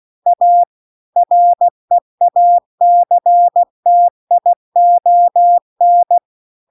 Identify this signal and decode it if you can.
Morse code, heard as a keyed tone, about 16 wpm, 700 Hz. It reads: A REACTION